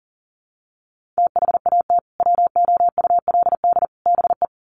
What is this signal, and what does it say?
Morse code, heard as a keyed tone, about 40 wpm, 700 Hz. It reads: THAT WOULD BE
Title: Sets of 3 Words 40wpm